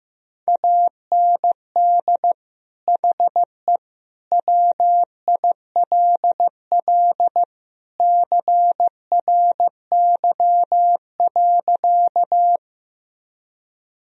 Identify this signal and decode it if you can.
Morse code, heard as a keyed tone, about 16 wpm, 700 Hz. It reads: AND HE WILL CRY.
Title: Sentences from Top 1000 Words - Review Part 2 - 15wpm